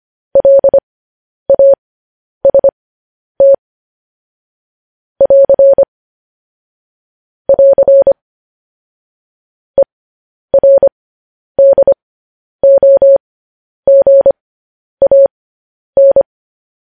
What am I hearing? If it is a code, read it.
Morse code, heard as a keyed tone, about 25 wpm, spaced as for 10 wpm, 550 Hz. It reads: LAST <AR> <AR> ERDOGAN
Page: news-headlines-in-morse-code-at-10-wpm